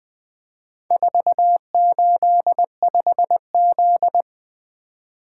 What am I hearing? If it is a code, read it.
Morse code, heard as a keyed tone, about 20 wpm, 700 Hz. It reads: 485Z
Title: Four Letter-Number Combinations - 20wpm